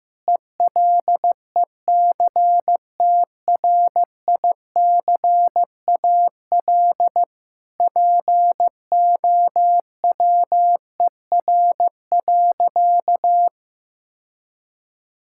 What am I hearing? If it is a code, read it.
Morse code, heard as a keyed tone, about 15 wpm, 700 Hz. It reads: ELECTRICAL POWER.
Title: Sentences from Top 1000 Words - Review Part 6 - 15wpm